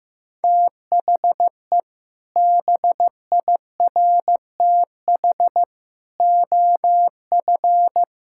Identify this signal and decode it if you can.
Morse code, heard as a keyed tone, about 15 wpm, 700 Hz. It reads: THE BIRTH OF
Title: Sentences from Top 1000 Words - Review Part 5 - 15wpm